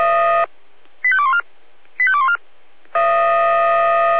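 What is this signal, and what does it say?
Five-tone selective call (ZVEI), heard as quick runs of five tones, tones of about 70 ms. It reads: 85214 85214
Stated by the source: ZVEI five-tone selective call 85244 received on SDR.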